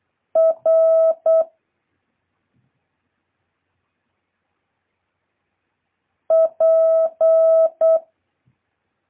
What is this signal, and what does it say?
Morse code, heard as a keyed tone, about 8 wpm, 650 Hz. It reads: R P
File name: jh8cbh / 2023cw